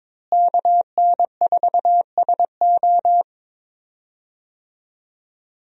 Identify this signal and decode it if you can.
Morse code, heard as a keyed tone, about 21 wpm, 700 Hz. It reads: KN4SO
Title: USA Call Signs - 5 Characters - Repeated 3x - 22wpm